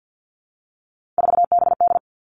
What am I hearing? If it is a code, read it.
Morse code, heard as a keyed tone, about 50 wpm, 700 Hz. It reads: 4BD
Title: Three Challenging Letter-Number Combinations 50wpm